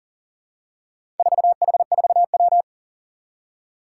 Morse code, heard as a keyed tone, about 39 wpm, 700 Hz. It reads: 4H4W
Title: Four Challenging Letter-Number Combinations 40wpm